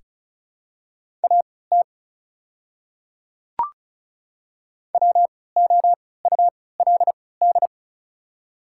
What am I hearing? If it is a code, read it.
Morse code, heard as a keyed tone, about 36 wpm, spaced as for 20 wpm, 690 Hz. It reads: AT WOULD
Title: 3x Character Spacing - Top 100 Words - 35wpm